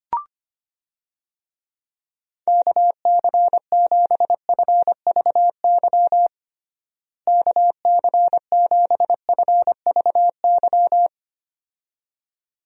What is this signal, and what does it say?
Morse code, heard as a keyed tone, about 25 wpm, 700 Hz. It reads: KC7FVY KC7FVY
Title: USA Call Signs - 6 Characters - Repeated 2x - 25wpm